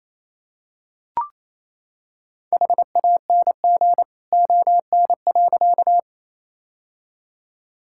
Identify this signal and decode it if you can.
Morse code, heard as a keyed tone, about 28 wpm, 700 Hz. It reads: HANG ON.
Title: Sentences from Top 1600 Words 28wpm